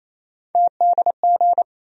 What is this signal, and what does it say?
Morse code, heard as a keyed tone, about 28 wpm, 700 Hz. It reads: TDG
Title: USA Call Sign Postfixes - 3 Letters 28wpm